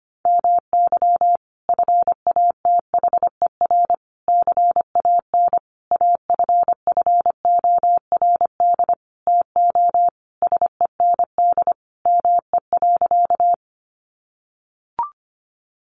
Morse code, heard as a keyed tone, about 25 wpm, 700 Hz. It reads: MY FATHER CAN AFFORD TO SEND ME.
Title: Sentences from Top 1500 Words 25wpm